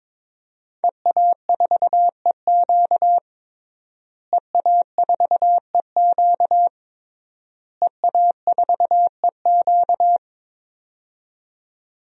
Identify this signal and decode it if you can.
Morse code, heard as a keyed tone, about 22 wpm, 700 Hz. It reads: EA4EQ EA4EQ EA4EQ
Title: DXCC-100 Call Signs - 5 Characters - Repeated 3x - 22wpm